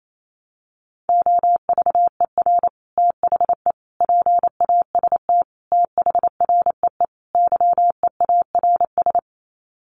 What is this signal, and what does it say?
Morse code, heard as a keyed tone, about 28 wpm, 700 Hz. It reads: OVER THE PAST THREE YEARS
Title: Sets of 5 Words Encore2 28wpm